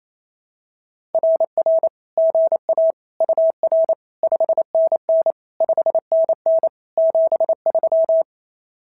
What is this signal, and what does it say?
Morse code, heard as a keyed tone, about 27 wpm, 660 Hz. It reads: RR GA UR 5NN 5NN 73
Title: SOTA QSO Practice 28wpm